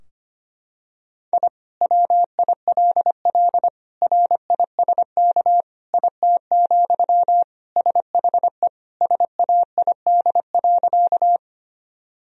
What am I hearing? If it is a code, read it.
Morse code, heard as a keyed tone, about 25 wpm, 700 Hz. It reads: I WILL RISK IT, SHE SAID.
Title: Sentences from Top 1000 Words - Review Part 4 - 25wpm